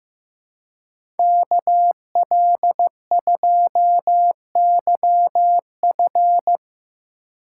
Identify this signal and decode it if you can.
Morse code, heard as a keyed tone, about 17 wpm, 700 Hz. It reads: KL2YF
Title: Speed Racing - USA 2x2 Call Signs 15wpm